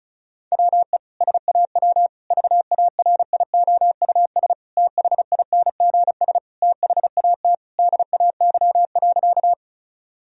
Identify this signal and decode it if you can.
Morse code, heard as a keyed tone, about 36 wpm, 700 Hz. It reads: WE SAW VARIOUS THINGS THAT DAY.
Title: Sentences from Top 1000 Words - Review Part 4 - 35wpm